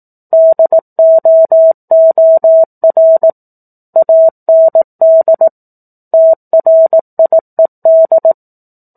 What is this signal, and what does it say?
Morse code, heard as a keyed tone, about 18 wpm, 650 Hz. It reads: DOOR AND TRIED